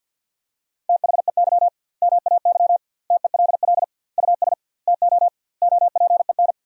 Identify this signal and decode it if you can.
Morse code, heard as a keyed tone, about 50 wpm, 700 Hz. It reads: THEY MAY TELL US TO OPEN
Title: 2x Word-Spacing - Sentences from Top 300 Words - 50wpm